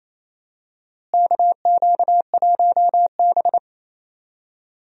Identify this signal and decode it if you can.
Morse code, heard as a keyed tone, about 28 wpm, 700 Hz. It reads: KQ1B
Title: USA Call Signs - 4 Characters - Repeated 2x - 28wpm